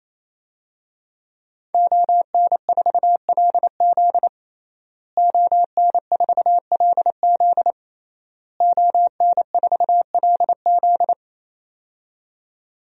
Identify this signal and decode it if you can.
Morse code, heard as a keyed tone, about 28 wpm, 700 Hz. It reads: ON4LZ ON4LZ ON4LZ
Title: DXCC-100 Call Signs - 5 Characters - Repeated 3x - 28wpm